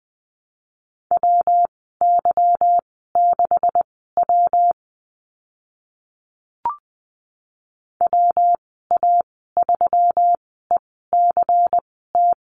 Morse code, heard as a keyed tone, about 20 wpm, spaced as for 15 wpm, 700 Hz. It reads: WY6W WA3ECT
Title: 147 Lesson 24g - Letter y - Call Signs 20wpm